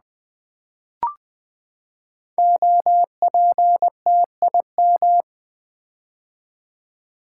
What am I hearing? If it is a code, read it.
Morse code, heard as a keyed tone, about 20 wpm, 700 Hz. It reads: OPTIM
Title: Build A Word - 5 Letters At A Time - 10-Letter Words - 20wpm